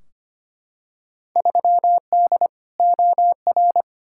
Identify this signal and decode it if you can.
Morse code, heard as a keyed tone, about 25 wpm, 700 Hz. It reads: 3D OR
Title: Field Day Exchanges - 25wpm